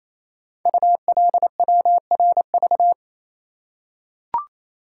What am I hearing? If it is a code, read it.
Morse code, heard as a keyed tone, about 28 wpm, 700 Hz. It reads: ULWRV
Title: Code Groups - 5 Letters - 28wpm